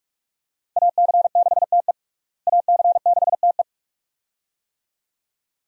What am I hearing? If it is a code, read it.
Morse code, heard as a keyed tone, about 45 wpm, 700 Hz. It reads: AC6TE AC6TE
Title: USA Call Signs - 5 Characters - Repeated 2x - 45wpm